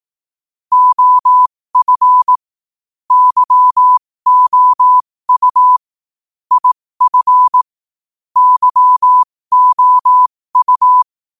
Morse code, heard as a keyed tone, about 18 wpm, spaced as for 15 wpm, 1000 Hz. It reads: OF YOU IF YOU
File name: qotdfiles / 15wpm